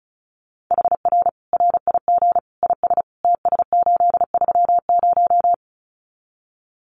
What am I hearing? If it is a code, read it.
Morse code, heard as a keyed tone, about 34 wpm, 700 Hz. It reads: HR RIG IS TS830
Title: Common QSO Phrases 35wpm